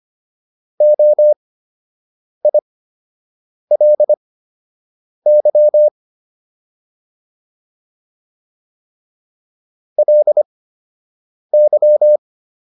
Morse code, heard as a keyed tone, about 25 wpm, spaced as for 7 wpm, 600 Hz. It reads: OILY LY